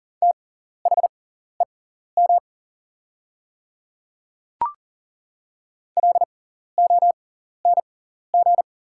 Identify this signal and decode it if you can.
Morse code, heard as a keyed tone, about 38 wpm, spaced as for 14 wpm, 700 Hz. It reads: THEM LONG
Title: 6x Character Spacing - Top 100 Words - 40wpm